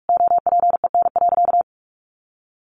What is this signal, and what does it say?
Morse code, heard as a keyed tone, about 44 wpm, 700 Hz. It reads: OPEN.
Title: Sentences from Top 1400 Words 45wpm